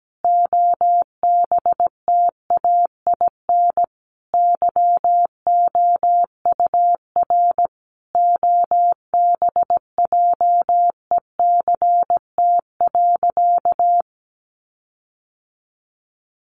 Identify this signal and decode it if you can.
Morse code, heard as a keyed tone, about 17 wpm, 700 Hz. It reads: OBTAIN YOUR OBJECT.
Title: Sentences from Top 1300 Words 17wpm